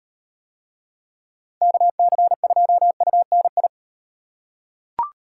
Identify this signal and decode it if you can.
Morse code, heard as a keyed tone, about 34 wpm, 700 Hz. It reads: KC2UNI